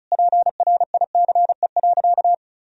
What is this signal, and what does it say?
Morse code, heard as a keyed tone, about 36 wpm, 700 Hz. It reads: PRICE.
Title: Sentences from Top 1000 Words - Review Part 4 - 35wpm